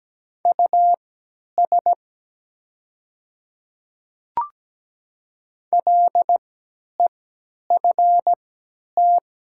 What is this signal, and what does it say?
Morse code, heard as a keyed tone, about 17 wpm, spaced as for 10 wpm, 700 Hz. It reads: US LEFT
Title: 3x Character Spacing - Top 500 Words - 17wpm